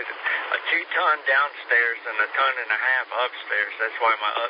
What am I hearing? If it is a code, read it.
Emergency Alert System (SAME) tones, heard as harsh buzzing bursts, with speech. It is heard from 0.4 to 3.9 s.